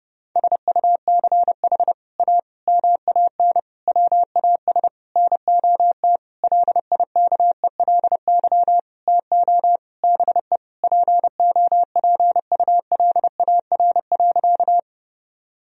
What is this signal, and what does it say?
Morse code, heard as a keyed tone, about 30 wpm, 700 Hz. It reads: SUCH A MAN WAS NOT LIKELY TO BE POPULAR.
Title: Sentences from Top 1000 Words - Review Part 4 - 30wpm